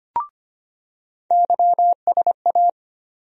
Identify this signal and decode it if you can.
Morse code, heard as a keyed tone, about 25 wpm, 700 Hz. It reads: YSA
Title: USA Call Sign Postfixes - 3 Letters 25wpm